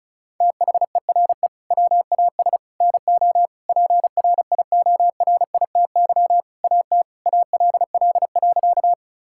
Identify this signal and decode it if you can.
Morse code, heard as a keyed tone, about 34 wpm, 700 Hz. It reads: THERE WAS NO PRIORITY AT ALL.
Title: Sentences from Top 1600 Words 35wpm